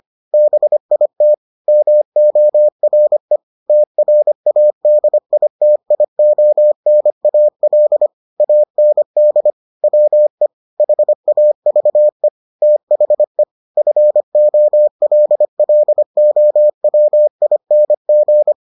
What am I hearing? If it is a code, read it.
Morse code, heard as a keyed tone, about 25 wpm, 600 Hz. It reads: BIT MORE TRADITIONAL AND WE HAVE THE FOLLOWING